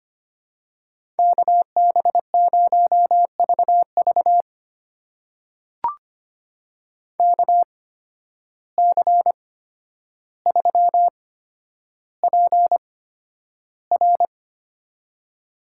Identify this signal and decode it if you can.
Morse code, heard as a keyed tone, about 25 wpm, spaced as for 11 wpm, 700 Hz. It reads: KB0VV KC3PR
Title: USA Call Signs - 5 Characters - 8x Character Spacing - 25wpm